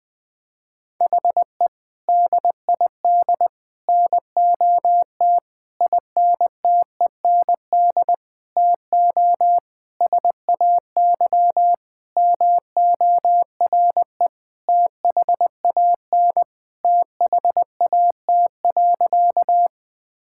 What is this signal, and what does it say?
Morse code, heard as a keyed tone, about 21 wpm, 700 Hz. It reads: HE DID NOT INTEND TO SAY MORE THAN THAT.
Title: Sentences from Top 1600 Words 20wpm